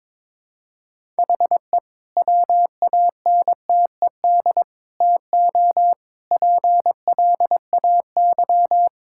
Morse code, heard as a keyed tone, about 22 wpm, 700 Hz. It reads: HE WANTED TO PLAY
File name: practice / 22wpm / Sentences with 3-Word Phrases - 22wpm